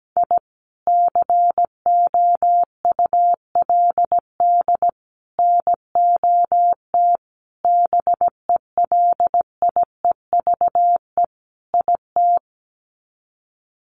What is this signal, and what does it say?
Morse code, heard as a keyed tone, about 17 wpm, 700 Hz. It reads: I COULD NOT BELIEVE IT
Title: Sets of 5 Words Encore 17wpm